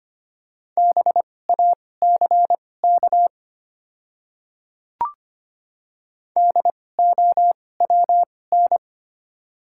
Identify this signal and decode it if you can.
Morse code, heard as a keyed tone, about 25 wpm, spaced as for 18 wpm, 700 Hz. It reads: BACK DOWN
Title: 2x Character Spacing - Top 200 Words - 25wpm